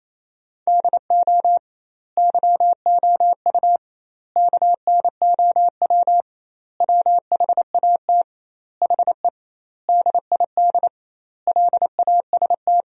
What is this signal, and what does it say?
Morse code, heard as a keyed tone, about 28 wpm, 700 Hz. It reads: DO YOU KNOW WHAT HE DID LAST
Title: 2x Word-Spacing - Sentences from Top 300 Words - 28wpm